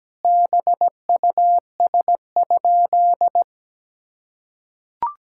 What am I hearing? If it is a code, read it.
Morse code, heard as a keyed tone, about 17 wpm, 700 Hz. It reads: BUS?
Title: Sentences from Top 1000 Words - Review Part 6 - 17wpm